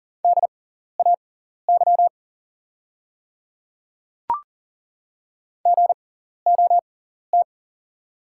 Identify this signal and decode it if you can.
Morse code, heard as a keyed tone, about 40 wpm, spaced as for 14 wpm, 690 Hz. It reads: DAY GOT
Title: 6x Character Spacing - Top 500 Words - 40wpm